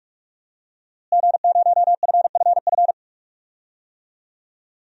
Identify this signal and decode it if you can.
Morse code, heard as a keyed tone, about 46 wpm, 700 Hz. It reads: G0FFF
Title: DXCC-100 Call Signs - 5 Characters - Repeated 3x - 45wpm